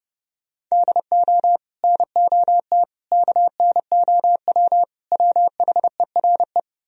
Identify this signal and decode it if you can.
Morse code, heard as a keyed tone, about 40 wpm, 700 Hz. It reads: DO NOT KNOW WHERE